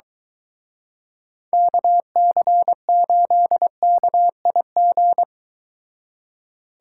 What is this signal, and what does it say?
Morse code, heard as a keyed tone, about 23 wpm, 700 Hz. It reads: KC8KIG